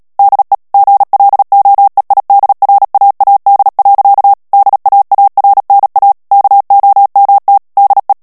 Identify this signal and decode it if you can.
Morse code, heard as a keyed tone, about 37 wpm, 800 Hz. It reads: DE GLOEIDRAAD. DAARNA KOMT DE
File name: morse mp3 / HOLLANDAIS texte clair 2 vitesse 35 mots minute